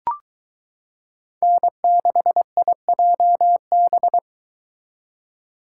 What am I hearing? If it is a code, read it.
Morse code, heard as a keyed tone, about 22 wpm, 700 Hz. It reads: N6IJB